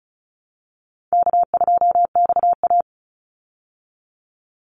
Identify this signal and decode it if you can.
Morse code, heard as a keyed tone, about 34 wpm, 700 Hz. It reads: K2XA
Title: Easy Call Signs Repeated 3x - 35wpm